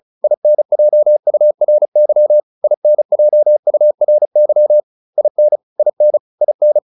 Morse code, heard as a keyed tone, about 34 wpm, 600 Hz. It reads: INJURY INJURY IN IN IN
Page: advanced-practice-files